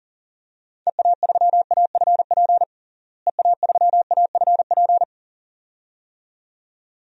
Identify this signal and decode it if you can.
Morse code, heard as a keyed tone, about 37 wpm, 700 Hz. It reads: EA3AFP EA3AFP
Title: DXCC-100 Call Signs - 6 Characters - Repeated 2x - 40wpm